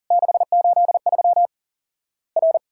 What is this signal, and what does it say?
Morse code, heard as a keyed tone, about 40 wpm, 700 Hz. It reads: 683 R
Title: Worked All Europe DX Contest - QTC Exchange - 40wpm